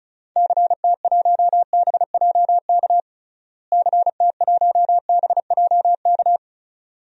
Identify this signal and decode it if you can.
Morse code, heard as a keyed tone, about 36 wpm, 700 Hz. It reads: CT1BJK CT1BJK
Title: DXCC-100 Call Signs - 6 Characters - Repeated 2x - 35wpm